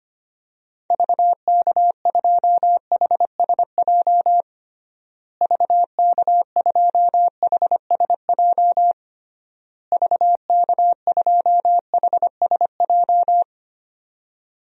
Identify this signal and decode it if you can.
Morse code, heard as a keyed tone, about 25 wpm, 700 Hz. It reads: VK2HSJ VK2HSJ VK2HSJ
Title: DXCC-100 Call Signs - 6 Characters - Repeated 3x - 25wpm